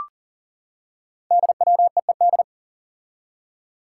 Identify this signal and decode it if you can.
Morse code, heard as a keyed tone, about 42 wpm, 700 Hz. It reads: DWEED
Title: Code Groups - 5 Letter:Numbers - 40wpm